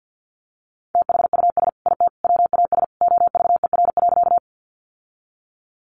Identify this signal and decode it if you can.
Morse code, heard as a keyed tone, about 50 wpm, 700 Hz. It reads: THUS IT WAS OVER.
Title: Sentences from Top 800 Words 50wpm